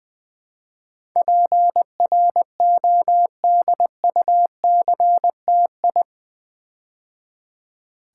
Morse code, heard as a keyed tone, about 20 wpm, 700 Hz. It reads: PRODUCTI
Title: Build A Word - 4 Letters At A Time - 12-Letter Words - 20wpm